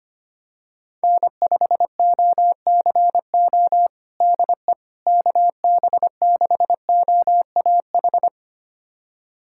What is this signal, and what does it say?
Morse code, heard as a keyed tone, about 25 wpm, 700 Hz. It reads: N5OCO DE KB6OAH
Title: USA Callsign DE Callsign 25wpm